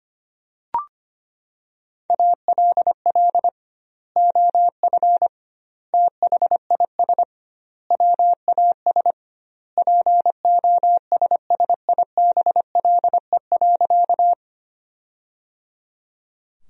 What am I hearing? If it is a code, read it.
Morse code, heard as a keyed tone, about 25 wpm, 700 Hz. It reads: ALL OF THIS WAS POSSIBLE.
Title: 2x Word-Spacing - Sentences from Top 500 Words - 25wpm